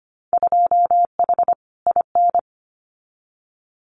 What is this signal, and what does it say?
Morse code, heard as a keyed tone, about 25 wpm, 700 Hz. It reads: 2H IN
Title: Winter Field Day Exchanges Encore 25wpm